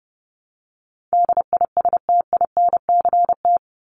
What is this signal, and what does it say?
Morse code, heard as a keyed tone, about 30 wpm, 700 Hz. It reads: DISTINCT
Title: Top 1501-1750 Words 30wpm